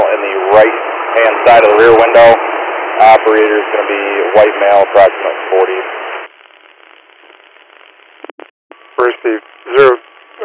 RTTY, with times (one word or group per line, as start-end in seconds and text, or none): none